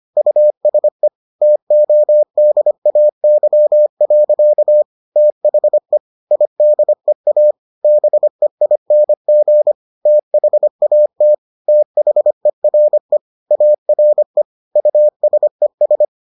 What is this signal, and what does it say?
Morse code, heard as a keyed tone, about 25 wpm, 600 Hz. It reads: USE TODAY. THE IDEA BEING THAT THERE ARE USES